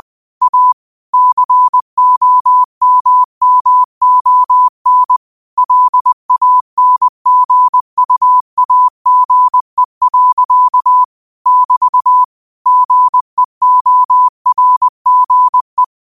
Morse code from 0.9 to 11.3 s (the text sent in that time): COMMON LANGUAGE.